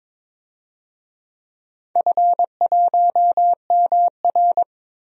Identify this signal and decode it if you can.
Morse code, heard as a keyed tone, about 22 wpm, 700 Hz. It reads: F1MR
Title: DXCC-100 Call Signs - 4 Characters - 22wpm